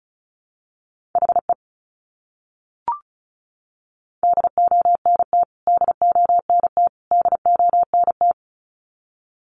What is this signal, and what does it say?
Morse code, heard as a keyed tone, about 34 wpm, 700 Hz. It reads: HE DONT DONT DONT
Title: Top 500 Words Repeated 35wpm